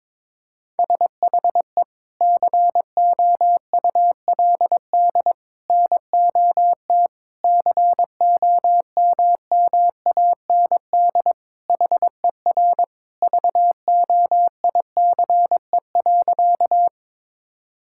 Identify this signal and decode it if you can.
Morse code, heard as a keyed tone, about 21 wpm, 700 Hz. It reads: SHE COULD NOT COMMAND HER VOICE.
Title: Sentences from Top 1600 Words 22wpm